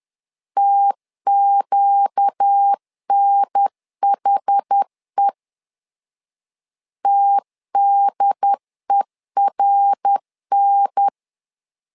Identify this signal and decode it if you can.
Morse code, heard as a keyed tone, about 10 wpm, 800 Hz. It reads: TQNHE TDERN
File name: LEZIONI / 4 Q 10wpm 2210